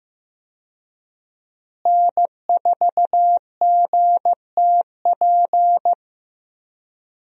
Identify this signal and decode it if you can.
Morse code, heard as a keyed tone, about 15 wpm, 700 Hz. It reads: N4GTP
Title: USA Call Signs - 5 Characters - Repeated 2x - 15wpm